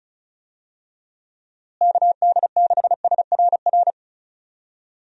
Morse code, heard as a keyed tone, about 34 wpm, 700 Hz. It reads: KD6SRR
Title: USA Call Signs - 6 Characters - Repeated 3x - 35wpm